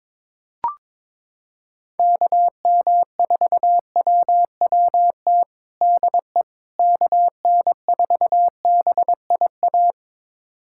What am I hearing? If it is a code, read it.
Morse code, heard as a keyed tone, about 22 wpm, 700 Hz. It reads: KM4WWT DE KN4BIA
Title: USA Callsign DE Callsign 22wpm